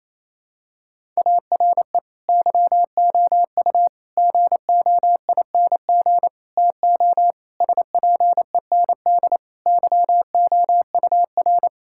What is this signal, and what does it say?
Morse code, heard as a keyed tone, about 28 wpm, 700 Hz. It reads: ARE YOU GOING TO SPEND YOUR
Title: Sentences from Top 1000 Words - Review Part 5 - 28wpm